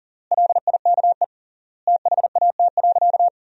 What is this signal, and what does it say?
Morse code, heard as a keyed tone, about 40 wpm, 700 Hz. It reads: LIKE THAT.
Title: 3x Word-Spacing - Sentences from Top 300 Words - 40wpm